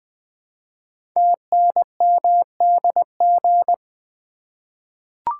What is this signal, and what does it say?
Morse code, heard as a keyed tone, about 20 wpm, 700 Hz. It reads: TNMDG
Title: Code Groups - 5 Letters - 20wpm